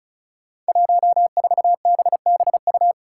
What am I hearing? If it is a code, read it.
Morse code, heard as a keyed tone, about 34 wpm, 700 Hz. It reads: 14BBU
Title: Five Challenging Letter-Number Combinations 35wpm